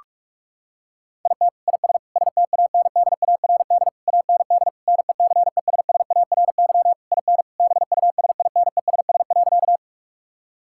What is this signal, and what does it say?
Morse code, heard as a keyed tone, about 43 wpm, 700 Hz. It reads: IT IS STANDARD AND NECESSARY IN BUSINESS.
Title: Sentences from Top 1000 Words - Review Part 6 - 45wpm